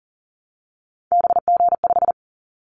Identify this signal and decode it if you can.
Morse code, heard as a keyed tone, about 40 wpm, 700 Hz. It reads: BG5
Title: Callsign Prefixes for Top 100 DXCC - 40wpm